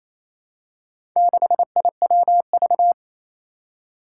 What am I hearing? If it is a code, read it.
Morse code, heard as a keyed tone, about 28 wpm, 700 Hz. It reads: 6IWV
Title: Four Challenging Letter-Number Combinations 28wpm